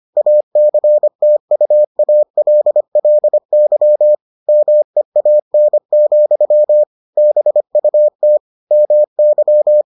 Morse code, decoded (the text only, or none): ACTUALLY MEAN, BUT MY